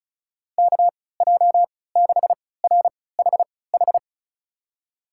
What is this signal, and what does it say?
Morse code, heard as a keyed tone, about 36 wpm, spaced as for 20 wpm, 700 Hz. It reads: KJ6RHH
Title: USA Call Signs - 6 Characters - 3x Character Spacing - 35wpm